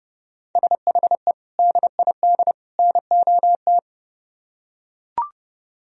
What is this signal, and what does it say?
Morse code, heard as a keyed tone, about 31 wpm, 700 Hz. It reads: SHE DID NOT